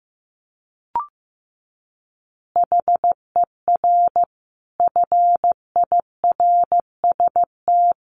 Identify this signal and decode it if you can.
Morse code, heard as a keyed tone, about 15 wpm, 700 Hz. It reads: HER FIRST
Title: Additional Easy Phrases 15wpm